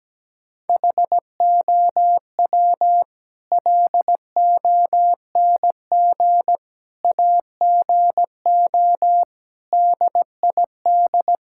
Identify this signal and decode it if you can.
Morse code, heard as a keyed tone, about 17 wpm, 700 Hz. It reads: HOW LONG AGO DID
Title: Sentences from Top 1000 Words - Review Part 6 - 17wpm